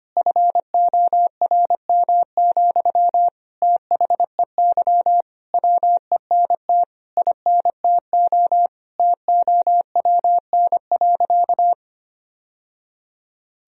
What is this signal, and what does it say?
Morse code, heard as a keyed tone, about 25 wpm, 700 Hz. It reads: FORM, THEY WENT INTO TOWN.
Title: Sentences from Top 1000 Words - Review Part 1 - 25wpm